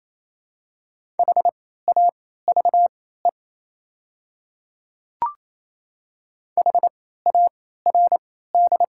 Morse code, heard as a keyed tone, about 28 wpm, spaced as for 16 wpm, 700 Hz. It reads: HAVE HARD
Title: 3x Character Spacing - Top 500 Words - 28wpm